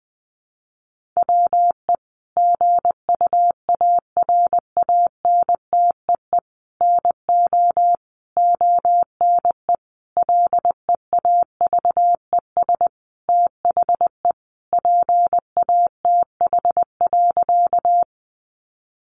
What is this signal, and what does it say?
Morse code, heard as a keyed tone, about 20 wpm, 700 Hz. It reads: WE GUARANTEE NO ONE LEAVES THE PATH.
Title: A Sound Of Thunder 20wpm